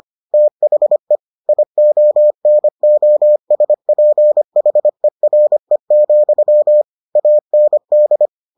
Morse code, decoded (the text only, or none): THE IONOSPHERE, AND